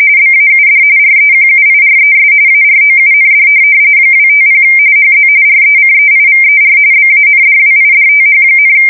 RTTY, about 45.45 baud. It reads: BAUD REQUIRES AROUND 250 HZ RECEIVER BANDWIDTH,
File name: fsk170 / teletype-RTTY